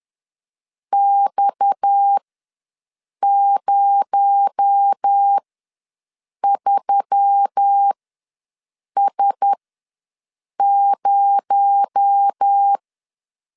Morse code, heard as a keyed tone, about 10 wpm, spaced as for 6 wpm, 800 Hz. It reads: X03S0
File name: LEZIONI / 6 PKX0 / 6 PKX0 ws 10wpm 4220